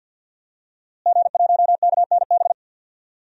Morse code, heard as a keyed tone, about 47 wpm, 700 Hz. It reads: G1KNB
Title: DXCC-100 Call Signs - 5 Characters - Repeated 2x - 50wpm